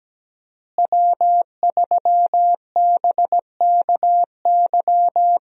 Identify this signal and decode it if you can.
Morse code, heard as a keyed tone, about 17 wpm, 700 Hz. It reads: W3BKY
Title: USA Call Signs - 5 Characters - Repeated 2x - 17wpm